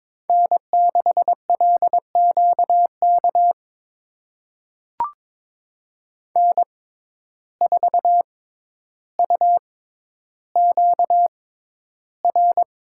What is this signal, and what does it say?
Morse code, heard as a keyed tone, about 21 wpm, spaced as for 11 wpm, 700 Hz. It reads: N6LQK N4UQR
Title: USA Call Signs - 5 Characters - 6x Character Spacing - 22wpm